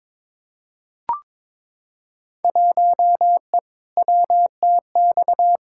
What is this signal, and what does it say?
Morse code, heard as a keyed tone, about 21 wpm, 700 Hz. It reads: 1E WTX
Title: Field Day Exchanges - 22wpm